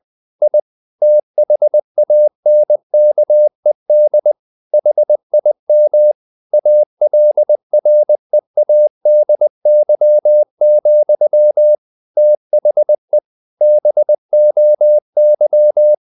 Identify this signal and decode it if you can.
Morse code, heard as a keyed tone, about 20 wpm, 600 Hz. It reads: I THANKED HIM ALREADY, THE BOY
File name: the old man and the sea / the old man and the sea part 1 20wpm